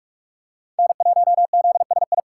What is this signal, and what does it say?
Morse code, heard as a keyed tone, about 44 wpm, 700 Hz. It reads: N1ZSI